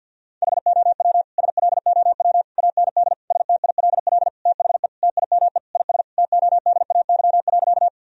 Morse code, heard as a keyed tone, about 50 wpm, 700 Hz. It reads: HOW SLOW AND STILL THE TIME IS TODAY.
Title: Sentences from Top 1000 Words - Review Part 2 - 50wpm